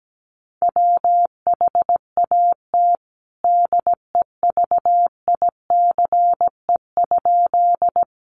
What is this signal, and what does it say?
Morse code, heard as a keyed tone, about 17 wpm, 700 Hz. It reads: WHAT DEVICE?
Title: Sentences from Top 1500 Words 17wpm